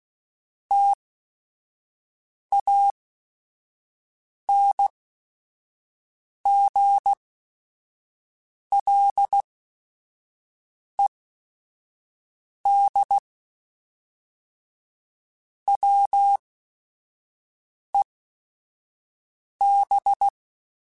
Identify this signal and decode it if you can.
Morse code, heard as a keyed tone, about 16 wpm, spaced as for 5 wpm, 800 Hz. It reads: TANGLED WEB
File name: Quotes / Quotes-mp3s / Deceive @5